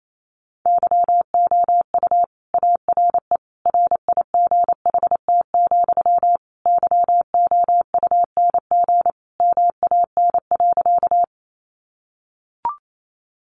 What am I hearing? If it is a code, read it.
Morse code, heard as a keyed tone, about 28 wpm, 700 Hz. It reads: YOU ARE RIGHT, YOUNG MAN.
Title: Sentences from Top 400 Words 28wpm